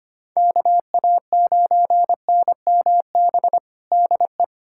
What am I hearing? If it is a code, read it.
Morse code, heard as a keyed tone, about 25 wpm, 700 Hz. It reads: KA9NMB DE
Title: USA Callsign DE Callsign 25wpm